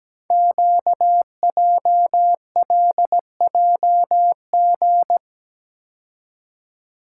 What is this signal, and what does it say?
Morse code, heard as a keyed tone, about 17 wpm, 700 Hz. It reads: QJLJG
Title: Code Groups - 5 Letters - 17wpm